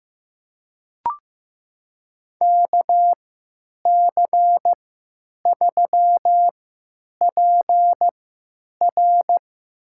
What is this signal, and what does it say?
Morse code, heard as a keyed tone, about 15 wpm, spaced as for 9 wpm, 700 Hz. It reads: KC3PR
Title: USA Call Signs - 3x Character Spacing - 15wpm